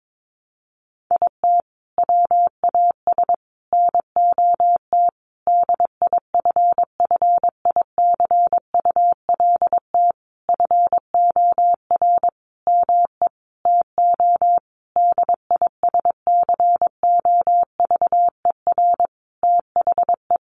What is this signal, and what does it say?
Morse code, heard as a keyed tone, about 22 wpm, 700 Hz. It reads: IT WAS NOT DIFFICULT FOR ME TO DISCOVER THE
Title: Sentences from Top 1200 Words 22wpm